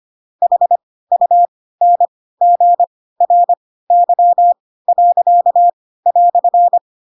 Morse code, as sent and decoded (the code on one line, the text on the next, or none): .... ..- -. --. .-. -.-- .-.-.- .-..-.
HUNGRY."